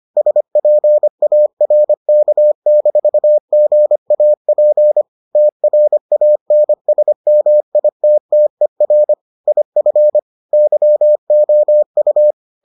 Morse code, heard as a keyed tone, about 25 wpm, 600 Hz. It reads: SPARK-GAP TRANSMITTER IF YOU